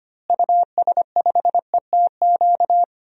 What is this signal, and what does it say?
Morse code, heard as a keyed tone, about 25 wpm, 700 Hz. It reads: US5ETQ
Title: DXCC-100 Call Signs - 6 Characters - Repeated 3x - 25wpm